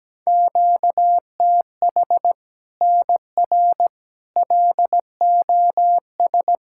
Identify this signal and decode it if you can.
Morse code, heard as a keyed tone, about 17 wpm, 700 Hz. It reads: QTH NR LOS
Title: Practice QSO - Exchange 1 - Beginning - 17wpm